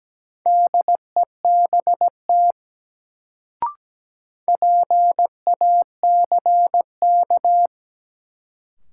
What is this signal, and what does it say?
Morse code, heard as a keyed tone, about 17 wpm, 700 Hz. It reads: DEBT PACK
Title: Sentences from Top 1300 Words - New Word Drill Alt - 17wpm